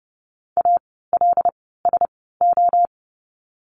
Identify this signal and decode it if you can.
Morse code, heard as a keyed tone, about 30 wpm, spaced as for 17 wpm, 700 Hz. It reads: ALSO
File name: Lesson 17f - Letter u - Words 30wpm